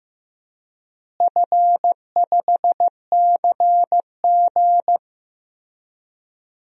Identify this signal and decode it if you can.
Morse code, heard as a keyed tone, about 15 wpm, 700 Hz. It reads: F5CG
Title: DXCC-100 Call Signs - 4 Characters - Repeated 3x - 15wpm